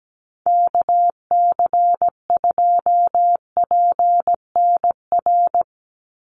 Morse code, heard as a keyed tone, about 17 wpm, 700 Hz. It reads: KC2PNR
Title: USA Call Signs 17wpm